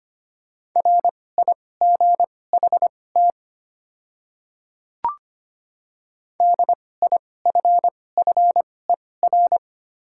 Morse code, heard as a keyed tone, about 25 wpm, spaced as for 18 wpm, 700 Hz. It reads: RIGHT DIFFER
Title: 2x Character Spacing - Top 200 Words - 25wpm